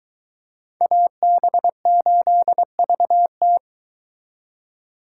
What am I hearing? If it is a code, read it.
Morse code, heard as a keyed tone, about 22 wpm, 700 Hz. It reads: AB8VT